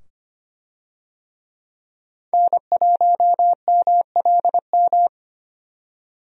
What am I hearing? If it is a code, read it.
Morse code, heard as a keyed tone, about 25 wpm, 690 Hz. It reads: N1MLM
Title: USA Call Signs - 5 Characters - 25wpm